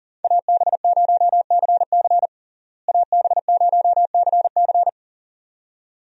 Morse code, heard as a keyed tone, about 40 wpm, 700 Hz. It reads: AB0CC AB0CC
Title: USA Call Signs - 5 Characters - Repeated 2x - 40wpm